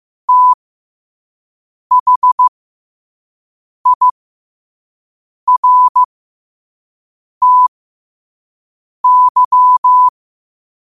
Morse code, heard as a keyed tone, about 15 wpm, spaced as for 5 wpm, 1000 Hz. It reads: THIRTY
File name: qotdfiles / 5wpm